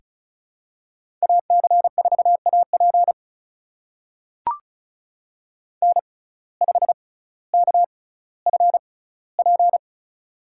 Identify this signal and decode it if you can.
Morse code, heard as a keyed tone, about 35 wpm, spaced as for 18 wpm, 700 Hz. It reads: AC4AP N5KFP
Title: USA Call Signs - 5 Characters - 6x Character Spacing - 35wpm